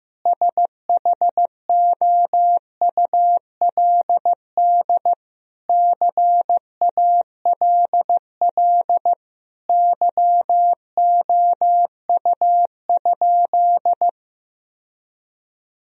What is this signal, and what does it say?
Morse code, heard as a keyed tone, about 15 wpm, 700 Hz. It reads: SHOULD CALL YOU?
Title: Sentences from Top 1000 Words - Review Part 3 - 15wpm